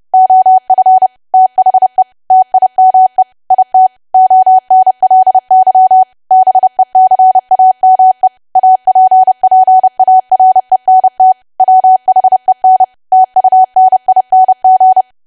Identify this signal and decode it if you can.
Morse code, heard as a keyed tone, about 30 wpm, 750 Hz. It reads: OF THE TIME IT ONLY BECAME APPARENT WHEN TUNING
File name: cwa / W1AW-Level3 / 30WPM